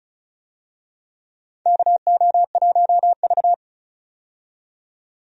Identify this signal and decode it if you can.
Morse code, heard as a keyed tone, about 36 wpm, 700 Hz. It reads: KO1V
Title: USA Call Signs - 4 Characters - 35wpm